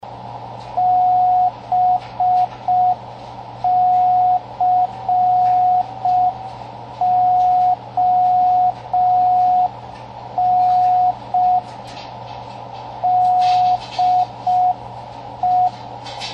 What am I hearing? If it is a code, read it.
Morse code, heard as a keyed tone, about 5 wpm, 730 Hz, with radio noise. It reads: BCON DE
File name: World's Slowest Beacon